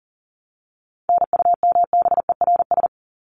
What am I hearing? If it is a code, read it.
Morse code, heard as a keyed tone, about 39 wpm, 700 Hz. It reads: NUMBERS
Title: ES:S Suffix - Top 100 Words with Base Words 40wpm